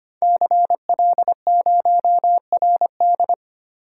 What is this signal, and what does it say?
Morse code, heard as a keyed tone, about 25 wpm, 700 Hz. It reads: CL0RD
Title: Code Groups - 5 Letter:Numbers - 25wpm